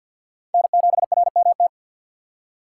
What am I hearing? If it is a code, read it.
Morse code, heard as a keyed tone, about 47 wpm, 700 Hz. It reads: N7RMT
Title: USA Call Signs - 6x Character Spacing - 50wpm